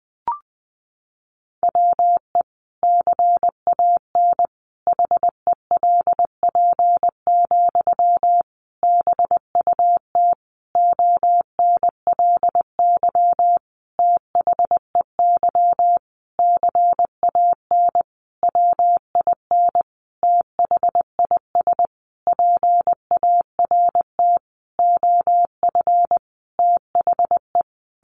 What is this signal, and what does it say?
Morse code, heard as a keyed tone, about 20 wpm, 700 Hz. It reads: WE CAN HELP, BUT ONLY THEY CAN WIN THIS PART OF THE
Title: Sentences from Top 1500 Words 20wpm